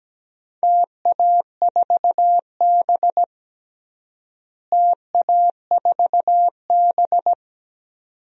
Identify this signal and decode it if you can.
Morse code, heard as a keyed tone, about 17 wpm, 700 Hz. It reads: TA4B TA4B
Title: DXCC-100 Call Signs - 4 Characters - Repeated 2x - 17wpm